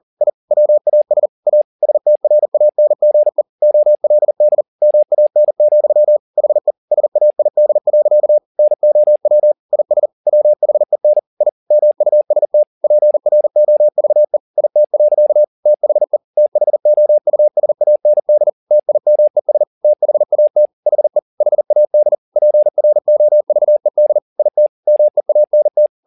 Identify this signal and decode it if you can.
Morse code, heard as a keyed tone, about 40 wpm, 600 Hz. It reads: I WAS A STRANGE OLD MAN, HE SAID. NOW IS WHEN I MUST PROVE IT. THE THOUSAND TIMES THAT HE HAD PROVED IT MEANT